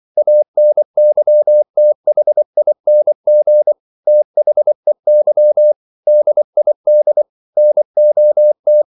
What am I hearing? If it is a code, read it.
Morse code, heard as a keyed tone, about 24 wpm, 600 Hz. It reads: ANYTHING THEY DID NOT